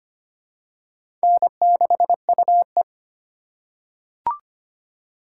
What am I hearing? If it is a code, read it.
Morse code, heard as a keyed tone, about 25 wpm, 700 Hz. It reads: N6UE
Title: USA Call Signs - 4 Characters - 4x Character Spacing - 25wpm